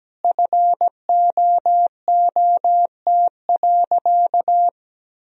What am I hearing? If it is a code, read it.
Morse code, heard as a keyed tone, about 17 wpm, 700 Hz. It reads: FOOT.
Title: Sentences from Top 1000 Words - Review Part 3 - 17wpm